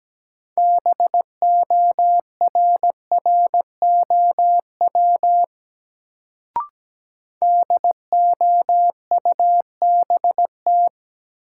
Morse code, heard as a keyed tone, about 17 wpm, 700 Hz. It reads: BORROW DOUBT
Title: Sentences from Top 1500 Words - New Word Drill - 17wpm